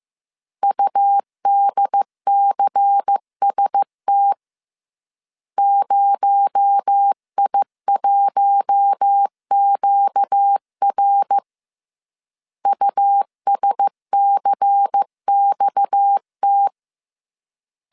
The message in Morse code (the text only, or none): UDCST 0I1QR USCXT